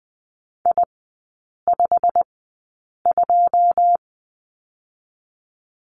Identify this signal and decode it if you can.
Morse code, heard as a keyed tone, about 20 wpm, spaced as for 8 wpm, 700 Hz. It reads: I52
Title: 121 Lesson 20e - Number 3 - Three Characters 20wpm